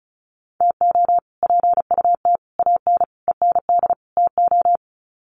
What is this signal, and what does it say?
Morse code, heard as a keyed tone, about 34 wpm, 700 Hz. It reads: TO PUT AN END TO
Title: Sets of 5 Words Encore2 35wpm